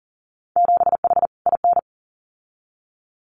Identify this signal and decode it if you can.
Morse code, heard as a keyed tone, about 40 wpm, 700 Hz. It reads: 7H IN
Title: Winter Field Day Exchanges Encore 40wpm